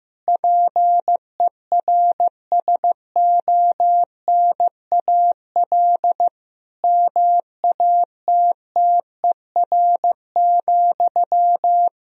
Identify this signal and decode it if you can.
Morse code, heard as a keyed tone, about 15 wpm, 700 Hz. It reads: PERSONAL MATTER,
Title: Sentences from Top 1000 Words - Review Part 4 - 15wpm